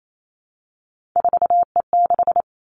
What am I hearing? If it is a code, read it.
Morse code, heard as a keyed tone, about 28 wpm, 700 Hz. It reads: 4E6
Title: Three Challenging Letter-Number Combinations 28wpm